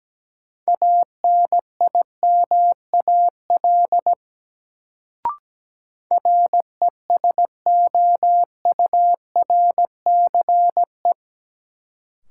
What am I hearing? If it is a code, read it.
Morse code, heard as a keyed tone, about 17 wpm, 700 Hz. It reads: ANIMAL RESOURCE
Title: Sentences from Top 1300 Words - New Word Drill Alt - 17wpm